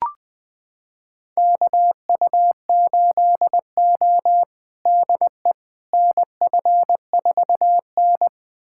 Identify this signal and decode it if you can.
Morse code, heard as a keyed tone, about 20 wpm, 700 Hz. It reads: KU8O DE NF4N
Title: USA Callsign DE Callsign - 4 Characters - 20wpm